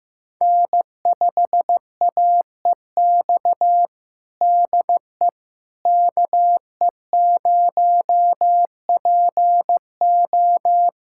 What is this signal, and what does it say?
Morse code, heard as a keyed tone, about 16 wpm, 700 Hz. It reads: N5AEX DE KE0PO
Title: USA Callsign DE Callsign - 5 Characters - 15wpm